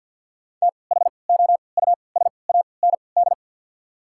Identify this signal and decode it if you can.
Morse code, heard as a keyed tone, about 54 wpm, spaced as for 29 wpm, 700 Hz. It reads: THOUSAND
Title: 3x Character Spacing - Top 500 Words - 50wpm